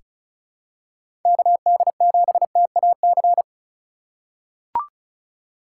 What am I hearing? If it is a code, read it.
Morse code, heard as a keyed tone, about 36 wpm, 700 Hz. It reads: KD7TAC
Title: USA Call Signs - 4x Character Spacing - 35wpm